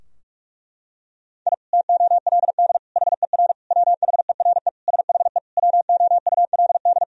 Morse code, heard as a keyed tone, about 46 wpm, 700 Hz. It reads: I TOLD HER WHERE SHE WOULD
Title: Sentences from Top 1000 Words - Review Part 2 - 45wpm